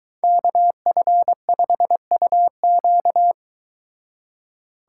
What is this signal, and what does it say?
Morse code, heard as a keyed tone, about 22 wpm, 700 Hz. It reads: KF5UQ